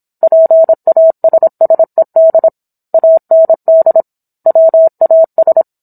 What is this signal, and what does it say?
Morse code, heard as a keyed tone, about 27 wpm, 650 Hz. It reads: PASSED AND WAS